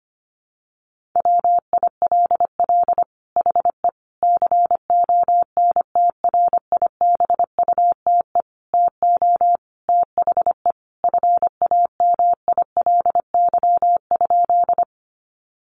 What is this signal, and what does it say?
Morse code, heard as a keyed tone, about 25 wpm, 700 Hz. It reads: WILL HE CONTRIBUTE TO THE FAMILY?
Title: Sentences from Top 1400 Words 25wpm